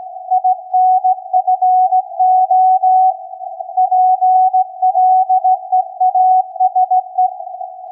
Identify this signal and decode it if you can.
Morse code, heard as a keyed tone, about 16 wpm, 740 Hz. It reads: INFO PLEASE
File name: iz7slz / VLF / SAQ 16-11-2022